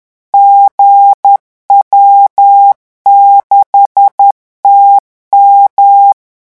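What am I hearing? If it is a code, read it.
Morse code, heard as a keyed tone, about 10 wpm, 800 Hz. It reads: GW6TM
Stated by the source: Morse revision exercise for French radio amateurs at 10 words per minute.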